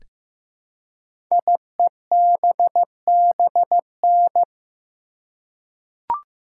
Morse code, heard as a keyed tone, about 15 wpm, 700 Hz. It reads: IEBBN
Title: Code Groups - 5 Letters - 15wpm